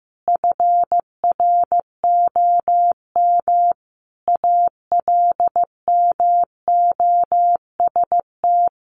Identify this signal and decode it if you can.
Morse code, heard as a keyed tone, about 15 wpm, 700 Hz. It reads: FROM ALMOST
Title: Sentences from Top 1400 Words 15wpm